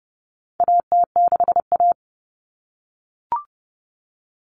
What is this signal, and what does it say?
Morse code, heard as a keyed tone, about 30 wpm, 700 Hz. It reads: AT6A
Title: Common Cut Numbers - 4-Digit 30wpm